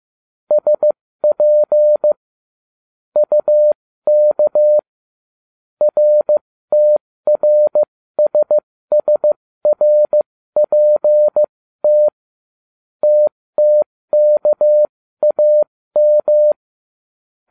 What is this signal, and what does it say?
Morse code, heard as a keyed tone, about 15 wpm, spaced as for 13 wpm, 600 Hz. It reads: SP UK RTRSSRPT TTKAM